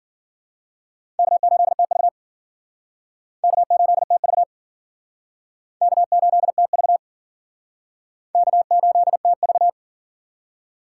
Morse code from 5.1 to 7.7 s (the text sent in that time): K8TV